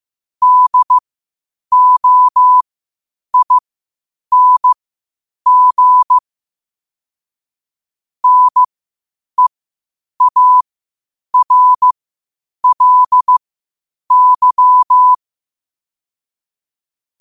Morse code, heard as a keyed tone, about 15 wpm, spaced as for 8 wpm, 1000 Hz. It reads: DOING NEARLY
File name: qotdfiles / 8wpm